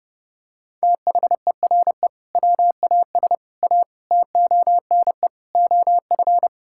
Morse code, heard as a keyed tone, about 30 wpm, 700 Hz. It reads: THERE WAS A TONE OF
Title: Sentences from Top 1600 Words 30wpm